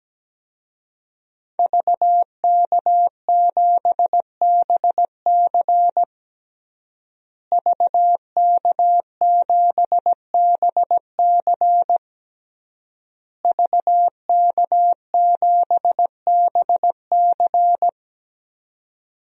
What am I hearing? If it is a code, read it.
Morse code, heard as a keyed tone, about 17 wpm, 700 Hz. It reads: VK7BC VK7BC VK7BC
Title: DXCC-100 Call Signs - 5 Characters - Repeated 3x - 17wpm